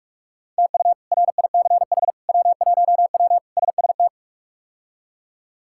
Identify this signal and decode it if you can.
Morse code, heard as a keyed tone, about 46 wpm, 700 Hz. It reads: TU RICH W1W SST
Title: K1USN SST QSO Practice - 45wpm